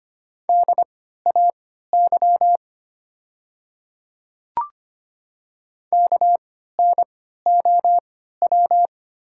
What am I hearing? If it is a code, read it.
Morse code, heard as a keyed tone, about 25 wpm, spaced as for 14 wpm, 700 Hz. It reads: DAY KNOW
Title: 3x Character Spacing - Top 100 Words - 25wpm